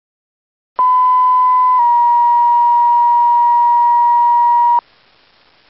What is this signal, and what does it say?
Two-tone page, heard as a one-second tone, then a three-second tone, about 1010 then 950 Hz. It is heard from 0.8 to 4.8 s.